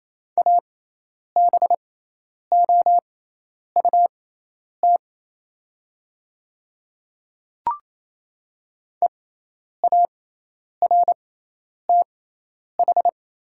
Morse code, heard as a keyed tone, about 27 wpm, spaced as for 10 wpm, 700 Hz. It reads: ABOUT EARTH
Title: 6x Character Spacing - Top 200 Words - 28wpm